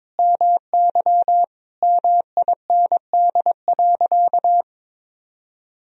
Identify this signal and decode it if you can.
Morse code, heard as a keyed tone, about 22 wpm, 700 Hz. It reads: MY MIND.
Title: Sentences from Top 1000 Words - Review Part 1 - 22wpm